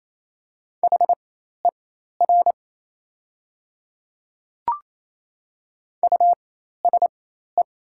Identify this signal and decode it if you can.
Morse code, heard as a keyed tone, about 28 wpm, spaced as for 13 wpm, 700 Hz. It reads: HER USE
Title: 4x Character Spacing - Top 100 Words - 28wpm